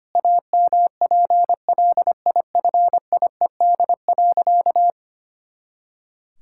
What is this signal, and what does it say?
Morse code, heard as a keyed tone, about 25 wpm, 700 Hz. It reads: AMPLIFIED.
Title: Pangram Challenge v2 - 25wpm